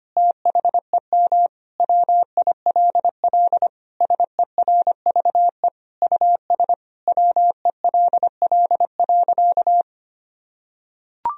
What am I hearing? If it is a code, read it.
Morse code, heard as a keyed tone, about 25 wpm, 700 Hz. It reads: THEM WILL SERVE US WELL.
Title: Sentences from Top 1000 Words - Review Part 4 - 25wpm